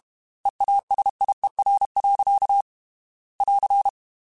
Morse code, heard as a keyed tone, about 32 wpm, 800 Hz. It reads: EASIER. <AR>
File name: Technical Subjects / Technical Subjects-mp3s / AND Gate @30